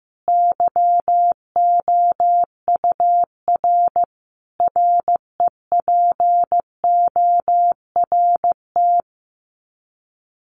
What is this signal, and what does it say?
Morse code, heard as a keyed tone, about 15 wpm, 700 Hz. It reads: YOUR REPORT
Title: Common QSO Phrases 15wpm